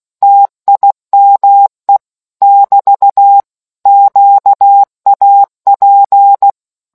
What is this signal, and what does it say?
Morse code, heard as a keyed tone, about 16 wpm, 800 Hz. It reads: TIME <BT> QAP